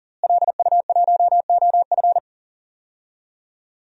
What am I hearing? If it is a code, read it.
Morse code, heard as a keyed tone, about 40 wpm, 700 Hz. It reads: LU1OF
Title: DXCC-100 Call Signs - 5 Characters - Repeated 3x - 40wpm